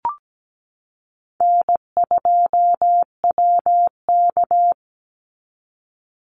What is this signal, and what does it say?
Morse code, heard as a keyed tone, about 17 wpm, 700 Hz. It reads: N2WK
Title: Easy Call Signs Repeated 2x - 17wpm